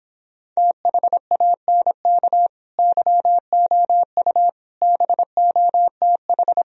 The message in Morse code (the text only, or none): THANK YOU BOTH